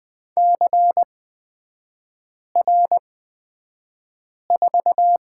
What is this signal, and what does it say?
Morse code, heard as a keyed tone, about 20 wpm, spaced as for 5 wpm, 700 Hz. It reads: CR4
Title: 236 Lesson 38e - Letter q - Three Characters 20wpm